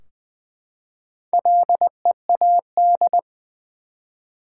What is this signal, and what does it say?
Morse code, heard as a keyed tone, about 20 wpm, 700 Hz. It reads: LEAD
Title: Four Letter Words - 20wpm